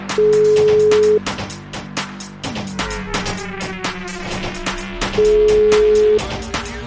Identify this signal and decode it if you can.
Ringback tone (Europe, 425 Hz), with noise.